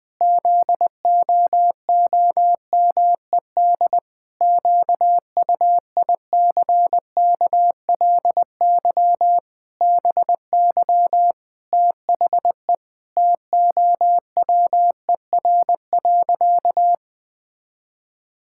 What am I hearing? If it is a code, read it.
Morse code, heard as a keyed tone, about 19 wpm, 700 Hz. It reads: ZOOMED QUICKLY BY THE TOWER.
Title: Pangram Challenge v2 - 20wpm